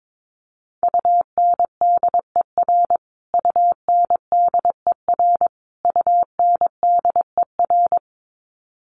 Morse code, heard as a keyed tone, about 22 wpm, 700 Hz. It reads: UNDER UNDER UNDER
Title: Top 200 Words Repeated 22wpm